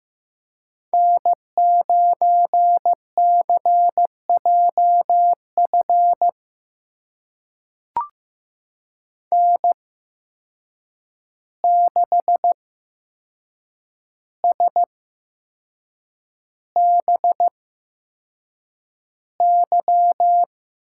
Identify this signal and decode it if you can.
Morse code, heard as a keyed tone, about 15 wpm, spaced as for 6 wpm, 700 Hz. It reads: N9CJF N6SBY
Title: USA Call Signs - 5 Characters - 8x Character Spacing - 15wpm